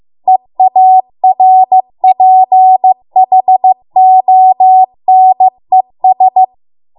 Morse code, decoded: EARPHONES